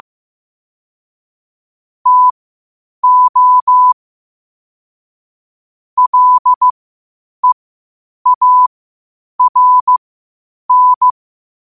Morse code, heard as a keyed tone, about 15 wpm, spaced as for 8 wpm, 1000 Hz. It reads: TO LEARN
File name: 8wpm